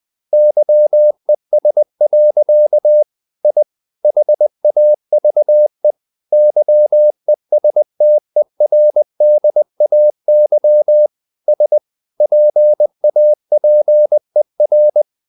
Morse code, heard as a keyed tone, about 20 wpm, 600 Hz. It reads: YES. I HAVE YESTERDAY S PAPER
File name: the old man and the sea / the old man and the sea part 1 20wpm